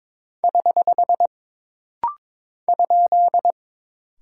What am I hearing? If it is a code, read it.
Morse code, heard as a keyed tone, about 22 wpm, 700 Hz. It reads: <HH> ?
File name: practice / 22wpm / Single Punctuation-Prosign - Rapid-Fire - 22wpm